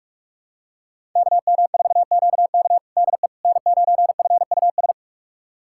Morse code, heard as a keyed tone, about 46 wpm, 700 Hz. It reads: KM4QK DE N9FUS
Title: USA Callsign DE Callsign - 5 Characters - 45wpm